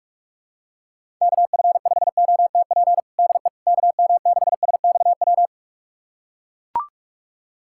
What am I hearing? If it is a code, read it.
Morse code, heard as a keyed tone, about 41 wpm, 700 Hz. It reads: KF5OTP DE KM6SXW
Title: USA Callsign DE Callsign 45wpm